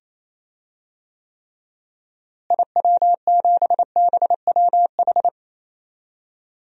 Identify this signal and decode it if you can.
Morse code, heard as a keyed tone, about 28 wpm, 700 Hz. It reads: IW7BWH
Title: DXCC-100 Call Signs - 6 Characters - Repeated 2x - 28wpm